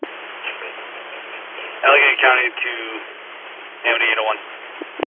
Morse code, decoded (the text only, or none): none